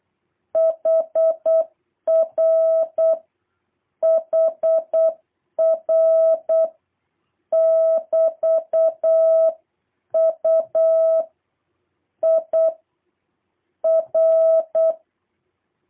Morse code, heard as a keyed tone, about 8 wpm, 650 Hz. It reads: HR HR <BT> U I R
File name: jh8cbh / 2023cw